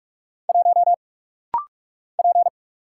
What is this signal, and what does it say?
Morse code, heard as a keyed tone, about 48 wpm, 690 Hz. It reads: 1 P
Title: Single Letter-Number - Rapid-Fire 45wpm